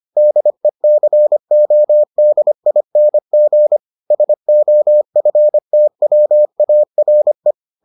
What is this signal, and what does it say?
Morse code, heard as a keyed tone, about 25 wpm, 600 Hz. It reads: DECODING SOFTWARE